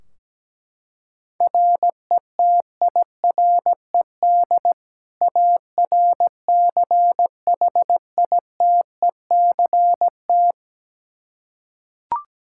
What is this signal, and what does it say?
Morse code, heard as a keyed tone, about 17 wpm, 700 Hz. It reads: RETIRED ARCHITECT
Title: Practice QSO - Exchange 3 - More Info - 17wpm